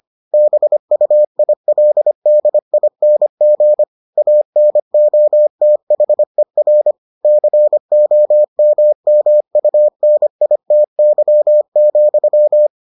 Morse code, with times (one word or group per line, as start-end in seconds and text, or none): 0.3-3.8 BUILDING
4.2-6.9 ANOTHER
7.2-12.7 COMMUNITY,